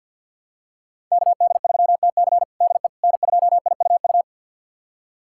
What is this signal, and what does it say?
Morse code, heard as a keyed tone, about 52 wpm, 700 Hz. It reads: KD3TC DE N2IUU
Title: USA Callsign DE Callsign - 5 Characters - 50wpm